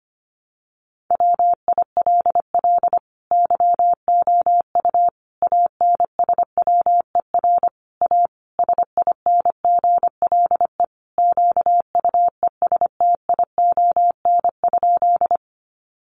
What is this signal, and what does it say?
Morse code, heard as a keyed tone, about 25 wpm, 700 Hz. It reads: WILL YOU ANSWER A SINGLE QUESTION?
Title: Sentences from Top 700 Words 25wpm